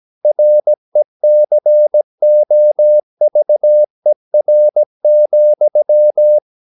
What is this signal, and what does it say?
Morse code, heard as a keyed tone, about 17 wpm, 600 Hz. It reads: RECOVER,
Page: Romanen Never af Ken Follet på engelsk.